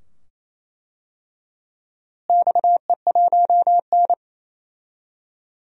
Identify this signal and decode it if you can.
Morse code, heard as a keyed tone, about 28 wpm, 700 Hz. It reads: XE1N
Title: DXCC-100 Call Signs - 4 Characters - Repeated 2x - 28wpm